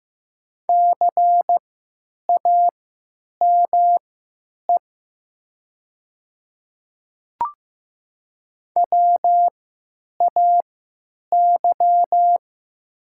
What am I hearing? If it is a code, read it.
Morse code, heard as a keyed tone, about 15 wpm, spaced as for 9 wpm, 700 Hz. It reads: CAME WAY
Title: 3x Character Spacing - Top 200 Words - 15wpm